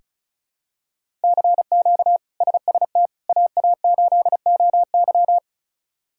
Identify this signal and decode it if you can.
Morse code, heard as a keyed tone, about 34 wpm, 700 Hz. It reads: CQ SST AA8OY
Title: K1USN SST QSO Practice - 35wpm